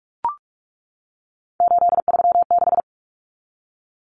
Morse code, heard as a keyed tone, about 44 wpm, 700 Hz. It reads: 836
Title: Common Cut Numbers - 3-Digit 45wpm